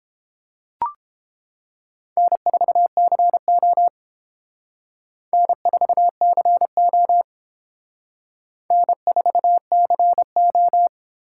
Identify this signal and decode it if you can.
Morse code, heard as a keyed tone, about 30 wpm, 700 Hz. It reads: N4CO N4CO N4CO